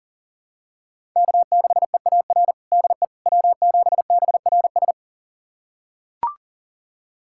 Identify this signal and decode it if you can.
Morse code, heard as a keyed tone, about 37 wpm, 700 Hz. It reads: K6EAR DE W7BRS
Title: USA Callsign DE Callsign - 5 Characters - 40wpm